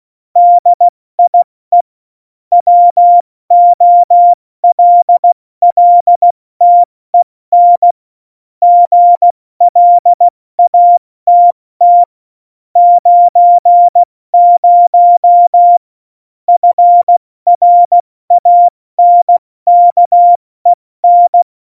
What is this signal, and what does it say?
Morse code, heard as a keyed tone, about 16 wpm, 700 Hz. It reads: DIE WOLLTEN GLATT 90 FRANKEN
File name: Morsekurs / Witze / Witz 07 Juli 2020, 70 F80